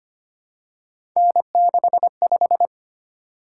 Morse code, heard as a keyed tone, about 25 wpm, 700 Hz. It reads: N65
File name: Three Letter-Number Combinations - 25wpm